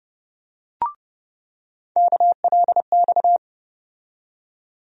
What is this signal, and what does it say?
Morse code, heard as a keyed tone, about 30 wpm, 700 Hz. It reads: KLX
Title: USA Call Sign Postfixes - 3 Letters 30wpm